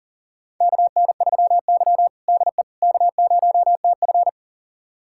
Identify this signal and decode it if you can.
Morse code, heard as a keyed tone, about 40 wpm, 700 Hz. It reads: KN3Y DE K0TF
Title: USA Callsign DE Callsign - 4 Characters - 40wpm